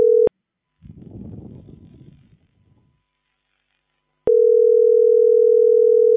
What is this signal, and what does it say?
Ringback tone (North America), 440 + 480 Hz.